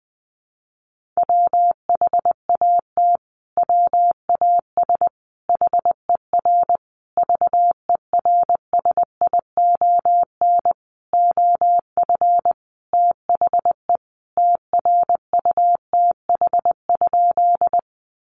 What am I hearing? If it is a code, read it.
Morse code, heard as a keyed tone, about 20 wpm, 700 Hz. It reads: WHAT WAS HER VERSION OF THE TRUTH?
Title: Sentences from Top 1400 Words 20wpm